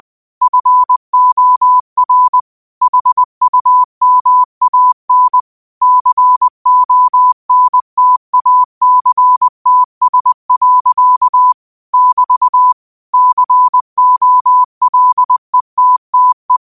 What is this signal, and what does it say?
Morse code, heard as a keyed tone, about 20 wpm, 1000 Hz. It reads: FOR HUMAN CONTACTS. <BT> COLETTE
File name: qotdfiles / 20wpm